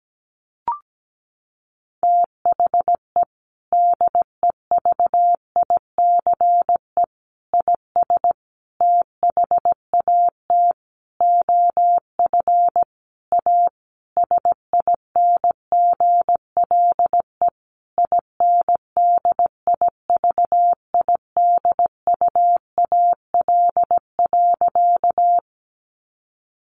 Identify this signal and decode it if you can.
Morse code, heard as a keyed tone, about 17 wpm, 700 Hz. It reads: THE DEVICE IS THAT OF A SINGLE INDIVIDUAL.
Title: Sentences from Top 1500 Words 17wpm